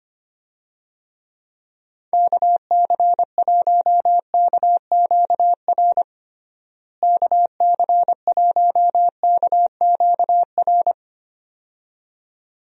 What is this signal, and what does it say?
Morse code, heard as a keyed tone, about 25 wpm, 700 Hz. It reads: KC1KQR KC1KQR
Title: USA Call Signs - 6 Characters - Repeated 2x - 25wpm